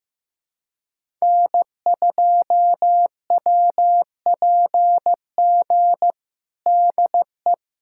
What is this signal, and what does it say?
Morse code, heard as a keyed tone, about 15 wpm, 700 Hz. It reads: N2WPG DE
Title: USA Callsign DE Callsign 15wpm